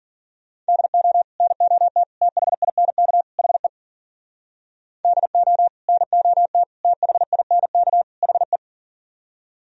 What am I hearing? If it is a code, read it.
Morse code, heard as a keyed tone, about 50 wpm, 700 Hz. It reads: DO NOT THINK HE DO NOT THINK HE